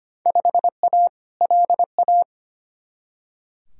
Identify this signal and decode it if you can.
Morse code, heard as a keyed tone, about 25 wpm, 700 Hz. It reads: 5A LA
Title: Field Day Exchanges Encore - 25wpm